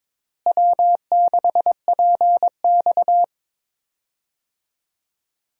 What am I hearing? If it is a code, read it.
Morse code, heard as a keyed tone, about 22 wpm, 700 Hz. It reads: W6PX
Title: USA Call Signs - 4 Characters - Repeated 3x - 22wpm